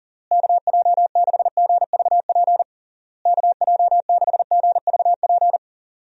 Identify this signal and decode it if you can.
Morse code, heard as a keyed tone, about 40 wpm, 700 Hz. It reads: KJ6GVP KJ6GVP
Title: USA Call Signs - 6 Characters - Repeated 3x - 40wpm